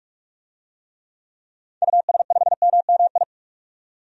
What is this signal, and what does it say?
Morse code, heard as a keyed tone, about 45 wpm, 700 Hz. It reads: US5MMI
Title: DXCC-100 Call Signs - 6 Characters - 45wpm